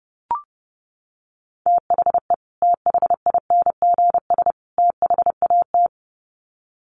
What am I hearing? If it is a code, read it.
Morse code, heard as a keyed tone, about 30 wpm, 700 Hz. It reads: THE THINGS THAT
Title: Sets of 3 Words 30wpm